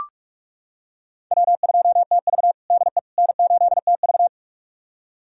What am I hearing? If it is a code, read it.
Morse code, heard as a keyed tone, about 43 wpm, 700 Hz. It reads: W2TV DE N8TV
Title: USA Callsign DE Callsign - 4 Characters - 45wpm